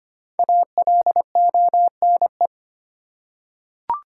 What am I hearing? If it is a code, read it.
Morse code, heard as a keyed tone, about 25 wpm, 700 Hz. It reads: ALONE
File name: 5 Letter Words - First 3 Letters Given - 25wpm